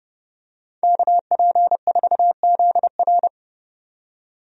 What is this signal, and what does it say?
Morse code, heard as a keyed tone, about 30 wpm, 700 Hz. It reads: KP4ZR
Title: USA Call Signs - 5 Characters - 30wpm